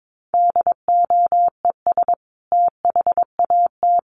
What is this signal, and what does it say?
Morse code, heard as a keyed tone, about 22 wpm, 700 Hz. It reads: DOES THAT
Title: Sentences from Top 1300 Words 22wpm